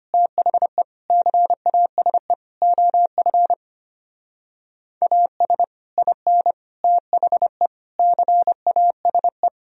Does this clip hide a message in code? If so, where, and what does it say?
Yes: Morse code from 0.1 to 9.5 s, heard as a keyed tone, about 32 wpm, 700 Hz. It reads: THE CASE OF AS IN THE CASE